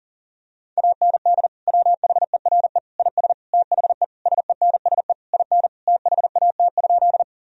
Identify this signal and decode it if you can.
Morse code, heard as a keyed tone, about 40 wpm, 700 Hz. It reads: AND WHERE IS THE SENSE IN THAT?
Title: Sentences from Top 1000 Words - Review Part 4 - 40wpm